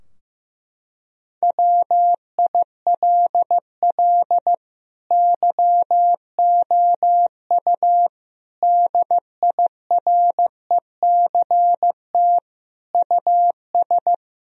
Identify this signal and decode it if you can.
Morse code, heard as a keyed tone, about 15 wpm, 700 Hz. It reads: WILL YOU DIRECT US
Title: Sentences from Top 1000 Words - Review Part 2 - 15wpm